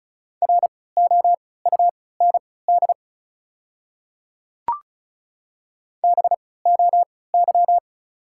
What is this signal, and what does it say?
Morse code, heard as a keyed tone, about 34 wpm, spaced as for 20 wpm, 700 Hz. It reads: ROUND BOY
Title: 3x Character Spacing - Top 500 Words - 35wpm